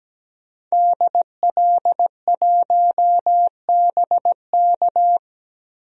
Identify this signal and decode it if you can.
Morse code, heard as a keyed tone, about 17 wpm, 700 Hz. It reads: DL1BK
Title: DXCC-100 Call Signs - 5 Characters - 17wpm